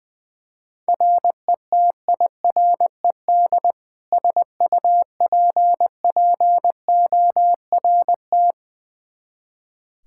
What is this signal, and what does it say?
Morse code, heard as a keyed tone, about 20 wpm, 700 Hz. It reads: RETIRED SUPPORT
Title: K1USN SST QSO Practice 20wpm